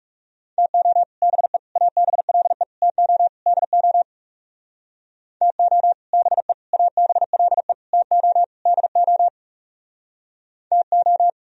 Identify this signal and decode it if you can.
Morse code, heard as a keyed tone, about 38 wpm, 700 Hz. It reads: TO BE ABLE TO DO TO BE ABLE TO DO TO